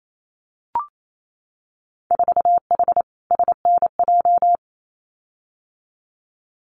Morse code, heard as a keyed tone, about 28 wpm, 700 Hz. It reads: 4H SNJ
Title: Winter Field Day Exchanges Encore 28wpm